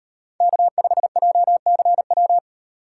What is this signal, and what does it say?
Morse code, heard as a keyed tone, about 34 wpm, 700 Hz. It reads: K5JCW